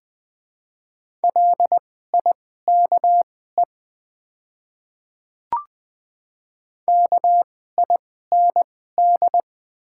Morse code, heard as a keyed tone, about 20 wpm, spaced as for 15 wpm, 700 Hz. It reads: LIKE KIND
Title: 2x Character Spacing - Top 200 Words - 20wpm